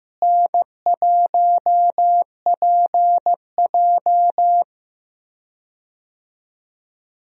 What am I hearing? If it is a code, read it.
Morse code, heard as a keyed tone, about 16 wpm, 700 Hz. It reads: N1PJ
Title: USA Call Signs - 4 Characters - Repeated 2x - 15wpm